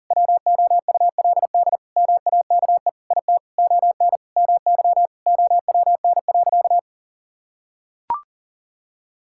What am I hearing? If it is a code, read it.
Morse code, heard as a keyed tone, about 37 wpm, 700 Hz. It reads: WOULD MAKE IT ON MY OWN.
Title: Sentences from Top 1000 Words - Review Part 6 - 40wpm